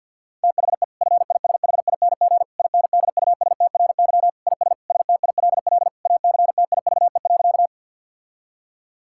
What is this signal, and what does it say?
Morse code, heard as a keyed tone, about 50 wpm, 700 Hz. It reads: THE FISHING INDUSTRY IS STILL ACTIVE.
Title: Sentences from Top 1000 Words - Review Part 6 - 50wpm